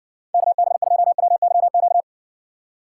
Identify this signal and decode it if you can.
Morse code, heard as a keyed tone, about 58 wpm, 700 Hz. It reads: KB2WYQ